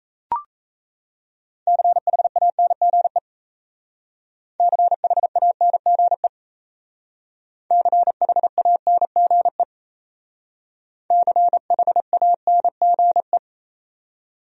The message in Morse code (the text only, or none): CHANGE CHANGE CHANGE CHANGE